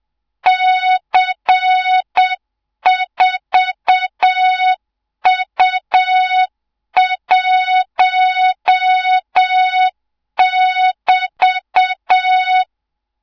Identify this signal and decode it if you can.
Morse code, heard as a keyed tone, about 7 wpm, 3700 Hz, with noise. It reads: C4U1<BT>
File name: Morze-HA2MN / hang / 24 GYAKORLAT